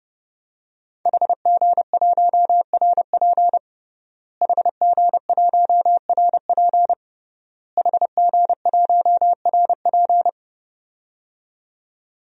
Morse code, heard as a keyed tone, about 29 wpm, 700 Hz. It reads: HG1RP HG1RP HG1RP
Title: DXCC-100 Call Signs - 5 Characters - Repeated 3x - 30wpm